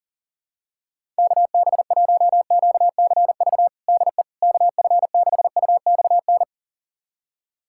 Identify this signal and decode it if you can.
Morse code, heard as a keyed tone, about 40 wpm, 700 Hz. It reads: KB1QCV DE KF6UXN
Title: USA Callsign DE Callsign - 6 Characters - 40wpm